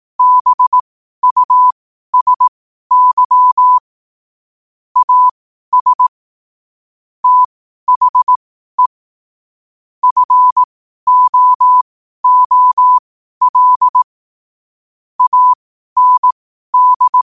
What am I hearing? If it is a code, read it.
Morse code, heard as a keyed tone, about 18 wpm, spaced as for 12 wpm, 1000 Hz. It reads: BUSY AS THE FOOL AND
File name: qotdfiles / 12wpm